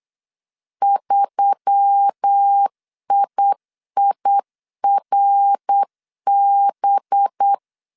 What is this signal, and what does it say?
Morse code, heard as a keyed tone, about 8 wpm, 800 Hz. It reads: 3IIRB
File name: CW / DET / 1 9 1500C S B 8wpm 5624